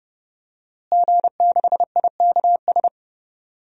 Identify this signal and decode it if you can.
Morse code, heard as a keyed tone, about 30 wpm, 700 Hz. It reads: G6IKS
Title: DXCC-100 Call Signs - 5 Characters - Repeated 2x - 30wpm